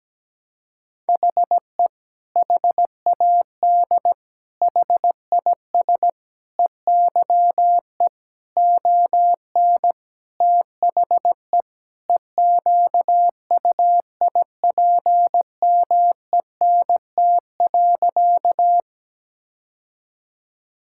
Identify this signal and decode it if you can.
Morse code, heard as a keyed tone, about 17 wpm, 700 Hz. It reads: HE HAD HIS EYE ON THE EQUIPMENT.
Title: Sentences from Top 1000 Words - Review Part 5 - 17wpm